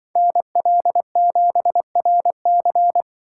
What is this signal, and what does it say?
Morse code, heard as a keyed tone, about 23 wpm, 700 Hz. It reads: NL7RC
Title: Speed Racing - USA 2x2 Call Signs 20wpm